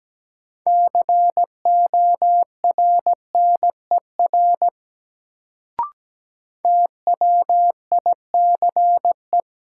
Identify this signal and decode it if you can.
Morse code, heard as a keyed tone, about 17 wpm, 700 Hz. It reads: CORNER TWICE
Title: Sentences from Top 1600 Words - New Word Drill - 17wpm